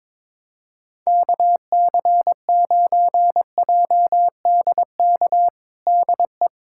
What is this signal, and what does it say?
Morse code, heard as a keyed tone, about 21 wpm, 700 Hz. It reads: KC9JDK DE
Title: USA Callsign DE Callsign - 6 Characters - 22wpm